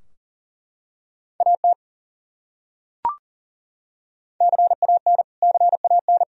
Morse code, heard as a keyed tone, about 37 wpm, 700 Hz. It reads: AT CAN CAN
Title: Top 100 Words Repeated 40wpm